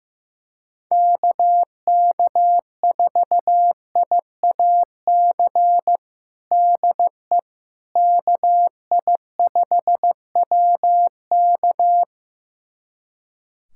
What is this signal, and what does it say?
Morse code, heard as a keyed tone, about 15 wpm, 700 Hz. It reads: KK4IAC DE KI5WK
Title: USA Callsign DE Callsign 15wpm